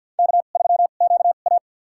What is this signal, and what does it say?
Morse code, heard as a keyed tone, about 52 wpm, spaced as for 36 wpm, 700 Hz. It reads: K3QA
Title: USA Call Signs - 4 Characters - 2x Character Spacing - 50wpm